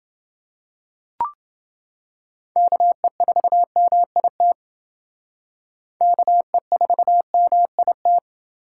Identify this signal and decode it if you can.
Morse code, heard as a keyed tone, about 29 wpm, 700 Hz. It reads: KE4MIT KE4MIT